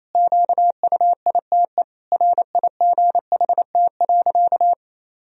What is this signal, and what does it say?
Morse code, heard as a keyed tone, about 27 wpm, 700 Hz. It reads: QUITE RIGHT.
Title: Sentences from Top 1000 Words - Review Part 2 - 28wpm